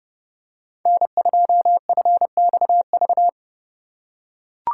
Morse code, heard as a keyed tone, about 30 wpm, 700 Hz. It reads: N2FXV